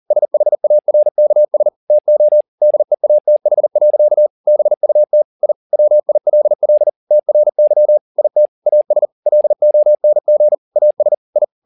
Morse code, heard as a keyed tone, about 40 wpm, 600 Hz. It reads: SHARKS TO DEATH. BUT I WILL TRY IT AS LONG AS I
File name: the old man and the sea 2 40wpm